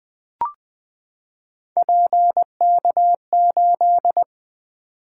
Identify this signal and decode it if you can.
Morse code, heard as a keyed tone, about 20 wpm, 700 Hz. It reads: PK8
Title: Three Letter-Number Combinations - 20wpm